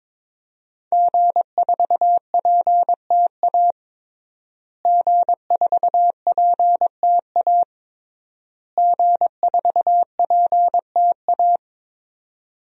Morse code, heard as a keyed tone, about 22 wpm, 700 Hz. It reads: G4PTA G4PTA G4PTA
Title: DXCC-100 Call Signs - 5 Characters - Repeated 3x - 22wpm